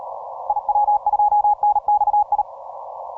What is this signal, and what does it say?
Morse code, heard as a keyed tone, about 40 wpm, 840 Hz. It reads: IW2NXI
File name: simple-file-list / TO6OK 10MCW